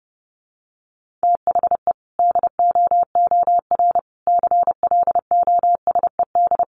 Morse code, heard as a keyed tone, about 30 wpm, 700 Hz. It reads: THE DOOR CLOSED
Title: Sentences from Top 1400 Words 30wpm